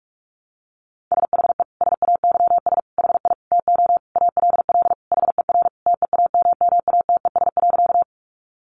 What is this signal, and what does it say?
Morse code, heard as a keyed tone, about 44 wpm, 700 Hz. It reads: SHE SAYS HI TO ALL HER TEAMMATES.
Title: Easy Phrases 45wpm